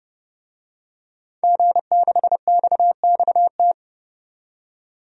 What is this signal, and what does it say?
Morse code, heard as a keyed tone, about 29 wpm, 700 Hz. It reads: G6XXT
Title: DXCC-100 Call Signs - 5 Characters - 30wpm